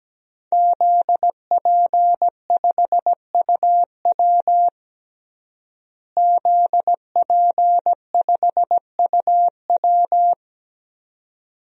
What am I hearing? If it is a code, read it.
Morse code, heard as a keyed tone, about 17 wpm, 700 Hz. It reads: ZP5UW ZP5UW
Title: DXCC-100 Call Signs - 5 Characters - Repeated 2x - 17wpm